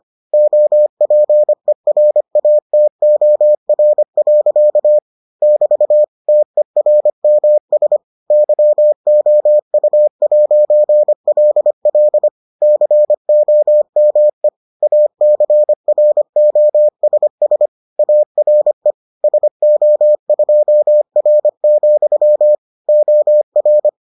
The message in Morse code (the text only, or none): OPERATOR. <BT> TERMS YOU'LL COME ACROSS ARE SO2R, OR